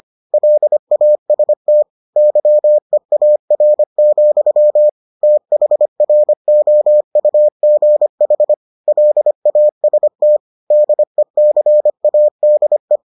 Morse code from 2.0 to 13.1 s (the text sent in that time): YEAR, THROUGH LAST DECADE